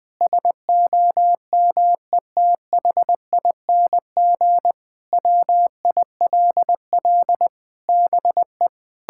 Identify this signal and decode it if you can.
Morse code, heard as a keyed tone, about 20 wpm, 700 Hz. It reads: SOMETHING WILL BE
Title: Sentences from Top 1600 Words 20wpm